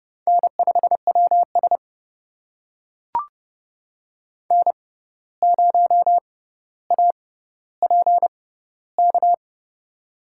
Morse code, heard as a keyed tone, about 30 wpm, spaced as for 14 wpm, 700 Hz. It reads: N5WS N0APK
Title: USA Call Signs - 6x Character Spacing - 30wpm